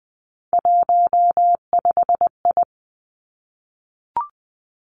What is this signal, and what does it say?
Morse code, heard as a keyed tone, about 20 wpm, 700 Hz. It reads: 15I
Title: Three Challenging Letter-Number Combinations 20wpm